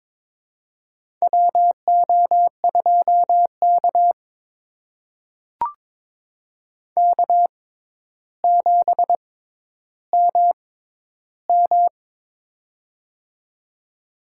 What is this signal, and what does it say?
Morse code, heard as a keyed tone, about 22 wpm, spaced as for 11 wpm, 700 Hz. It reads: WO2K K7MM
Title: USA Call Signs - 4 Characters - 6x Character Spacing - 22wpm